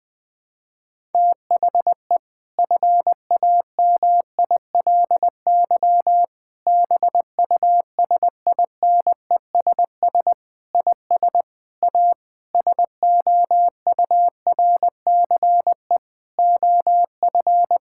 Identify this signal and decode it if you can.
Morse code, heard as a keyed tone, about 20 wpm, 700 Hz. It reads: THE FAMILY BUSINESS IS A SOURCE OF
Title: Sentences from Top 1000 Words - Review Part 6 - 20wpm